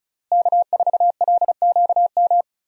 Morse code, heard as a keyed tone, about 36 wpm, 700 Hz. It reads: K4LQM
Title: USA Call Signs - 5 Characters - Repeated 3x - 35wpm